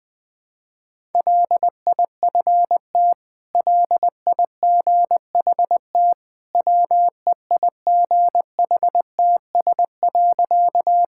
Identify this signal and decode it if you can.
Morse code, heard as a keyed tone, about 20 wpm, 700 Hz. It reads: LIFT LIGHT WEIGHTS.
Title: Easy Sentences with Hints - 20wpm